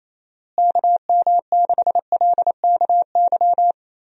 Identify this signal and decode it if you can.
Morse code, heard as a keyed tone, about 28 wpm, 700 Hz. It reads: KM6LKY
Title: USA Call Signs - 6 Characters - Repeated 2x - 28wpm